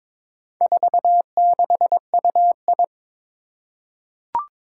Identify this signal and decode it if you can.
Morse code, heard as a keyed tone, about 22 wpm, 700 Hz. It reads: 46UI
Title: Four Challenging Letter-Number Combinations 22wpm